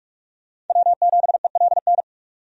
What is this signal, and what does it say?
Morse code, heard as a keyed tone, about 46 wpm, 700 Hz. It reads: W7ELN
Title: USA Call Signs - 5 Characters - 45wpm